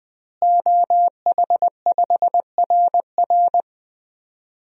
Morse code, heard as a keyed tone, about 20 wpm, 700 Hz. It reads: OH5RR
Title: DXCC-100 Call Signs - 5 Characters - Repeated 3x - 20wpm